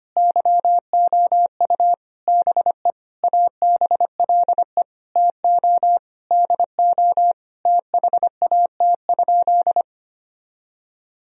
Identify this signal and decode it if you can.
Morse code, heard as a keyed tone, about 25 wpm, 700 Hz. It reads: YOU BE ABLE TO DO THAT?
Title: Sentences from Top 1000 Words - Review Part 2 - 25wpm